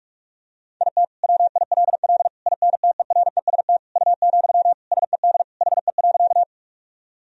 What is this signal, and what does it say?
Morse code, heard as a keyed tone, about 41 wpm, 700 Hz. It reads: IT WILL INTEREST U, SED HE.
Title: Sentences from Top 400 Words and Abbreviations 45wpm